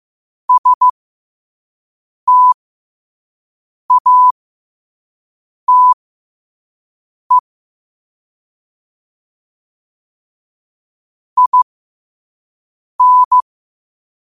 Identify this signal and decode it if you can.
Morse code, heard as a keyed tone, about 15 wpm, spaced as for 5 wpm, 1000 Hz. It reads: STATE IN
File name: 5wpm